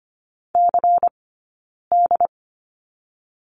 Morse code, heard as a keyed tone, about 25 wpm, 700 Hz. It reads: C D
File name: course / 25:10wpm / 092 Lesson 16d - Letter c - Two Characters 25wpm